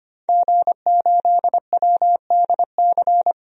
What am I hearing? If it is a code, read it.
Morse code, heard as a keyed tone, about 25 wpm, 690 Hz. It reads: G8WDC
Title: DXCC-100 Call Signs - 5 Characters - 25wpm